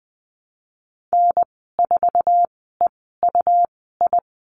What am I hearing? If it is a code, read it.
Morse code, heard as a keyed tone, about 20 wpm, spaced as for 15 wpm, 700 Hz. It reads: N4EUI
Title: 128 Lesson 21g - Number 6 - Call Signs 20wpm